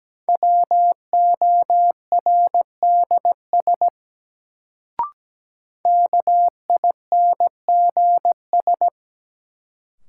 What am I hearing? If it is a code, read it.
Morse code, heard as a keyed tone, about 17 wpm, 700 Hz. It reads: WORDS KINGS
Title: Sentences from Top 1000 Words - Review Part 6 - New Word Drill - 17wpm